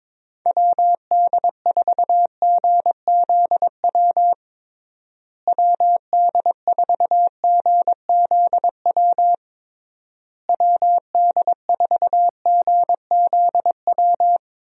Morse code, heard as a keyed tone, about 22 wpm, 700 Hz. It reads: WD4GZW WD4GZW WD4GZW
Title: USA Call Signs - 6 Characters - Repeated 3x - 22wpm